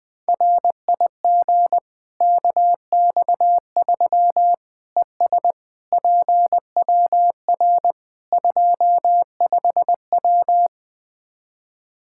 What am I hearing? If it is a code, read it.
Morse code, heard as a keyed tone, about 21 wpm, 700 Hz. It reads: RIG KX3 ES PWR 25W
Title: Practice QSO - Exchange 2 - Info - 20wpm